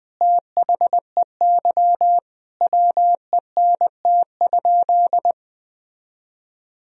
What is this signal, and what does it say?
Morse code, heard as a keyed tone, about 20 wpm, 700 Hz. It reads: THEY WENT?
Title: Sentences from Top 1000 Words - Review Part 3 - 20wpm